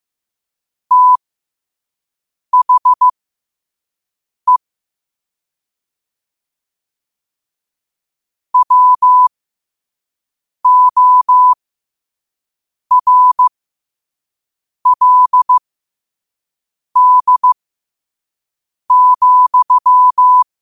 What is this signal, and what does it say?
Morse code, heard as a keyed tone, about 15 wpm, spaced as for 5 wpm, 1000 Hz. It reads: THE WORLD,
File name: qotdfiles / 5wpm